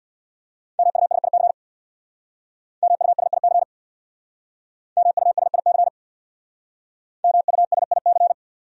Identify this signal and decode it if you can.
Morse code, heard as a keyed tone, about 62 wpm, 700 Hz. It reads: MUSIC MUSIC MUSIC MUSIC
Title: Speed-Racing Top 300 Words 50wpm